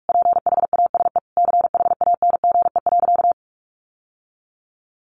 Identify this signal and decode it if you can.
Morse code, heard as a keyed tone, about 44 wpm, 700 Hz. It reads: PHASE CHANGE.
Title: Sentences from Top 1200 Words 45wpm